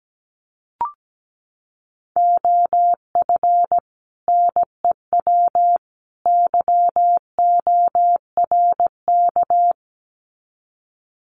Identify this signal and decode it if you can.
Morse code, heard as a keyed tone, about 17 wpm, 700 Hz. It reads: OF NEW YORK
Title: Sets of 3 Words 17wpm